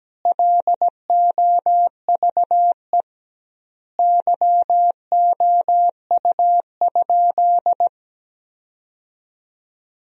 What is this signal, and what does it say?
Morse code, heard as a keyed tone, about 17 wpm, 700 Hz. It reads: LOVE YOU?
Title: 2x Word-Spacing - Sentences from Top 400 Words - 17wpm